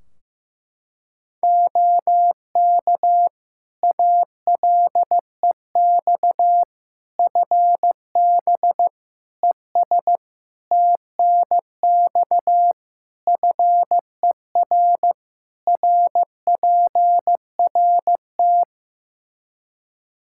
Morse code, heard as a keyed tone, about 16 wpm, 700 Hz. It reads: OK ALEX FB ES TNX FER RPRT
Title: Practice QSO - Exchange 2 - Info - 15wpm